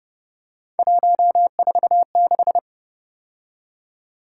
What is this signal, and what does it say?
Morse code, heard as a keyed tone, about 28 wpm, 700 Hz. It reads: 146